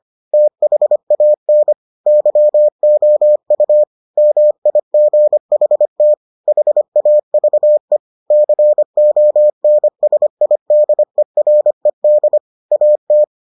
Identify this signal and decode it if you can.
Morse code, heard as a keyed tone, about 25 wpm, 600 Hz. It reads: THAN YOU MIGHT HAVE CONSIDERED AT